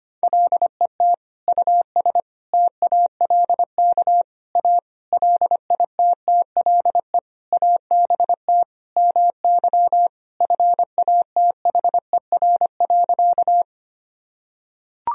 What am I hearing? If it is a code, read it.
Morse code, heard as a keyed tone, about 25 wpm, 700 Hz. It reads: LET US TALK A LITTLE ABT MY FATHER.
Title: Sentences from Top 400 Words and Abbreviations 25wpm